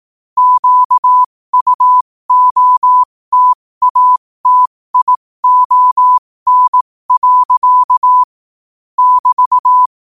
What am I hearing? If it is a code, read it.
Morse code, heard as a keyed tone, about 18 wpm, spaced as for 15 wpm, 1000 Hz. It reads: QUOTATION. <BT>
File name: qotdfiles / 15wpm